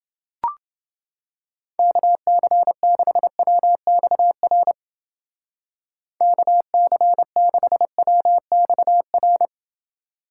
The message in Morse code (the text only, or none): KC6WXR KC6WXR